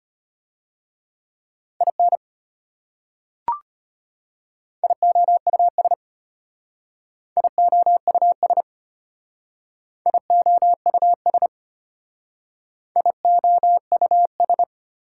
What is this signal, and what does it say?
Morse code, heard as a keyed tone, about 30 wpm, 700 Hz. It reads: IN IOUS IOUS IOUS IOUS
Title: Speed-Racing Prefixes and Suffixes 25wpm